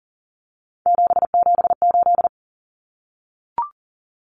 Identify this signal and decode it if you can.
Morse code, heard as a keyed tone, about 40 wpm, 700 Hz. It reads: 778
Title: Numbers - 3-Digit 40wpm